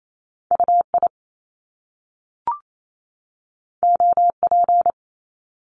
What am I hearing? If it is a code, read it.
Morse code, heard as a keyed tone, about 28 wpm, 700 Hz. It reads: UI OP
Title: Top Two Letter Combinations in Words 28wpm